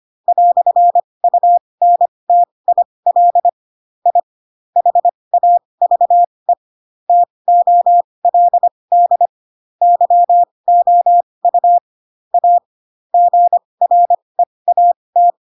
Morse code, heard as a keyed tone, about 25 wpm, spaced as for 20 wpm, 700 Hz. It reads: "UNTIL I HAVE TOLD YOU A GREAT
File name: Pinocchio / Pinocchio ch4 15 25 ik1hge